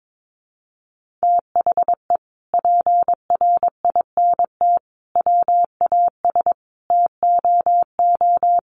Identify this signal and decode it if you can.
Morse code, heard as a keyed tone, about 22 wpm, 700 Hz. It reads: THE PRINT WAS TOO
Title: Sentences from Top 1400 Words 22wpm